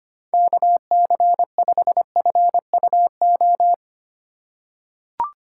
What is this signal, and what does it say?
Morse code, heard as a keyed tone, about 25 wpm, 700 Hz. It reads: KC5FUO
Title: USA Call Signs - 6 Characters - 3x Character Spacing - 25wpm